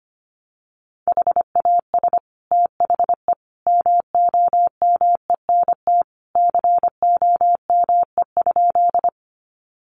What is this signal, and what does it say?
Morse code, heard as a keyed tone, about 25 wpm, 700 Hz. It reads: HAS THE MOMENT COME?
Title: Sentences from Top 1000 Words 25wpm